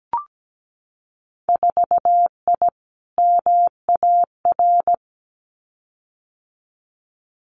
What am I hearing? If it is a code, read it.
Morse code, heard as a keyed tone, about 17 wpm, 700 Hz. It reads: 4I MAR
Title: Winter Field Day Exchanges Encore 17wpm